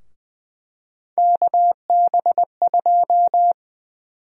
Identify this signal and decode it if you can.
Morse code, heard as a keyed tone, about 21 wpm, 700 Hz. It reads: KB2
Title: USA Call Sign Prefixes - 20wpm